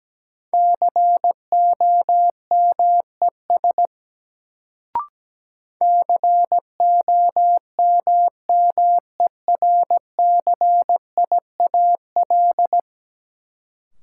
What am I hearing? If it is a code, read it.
Morse code, heard as a keyed tone, about 17 wpm, 700 Hz. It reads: COMES COMMERCIAL
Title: Sentences from Top 1000 Words - Review Part 5 - New Word Drill - 17wpm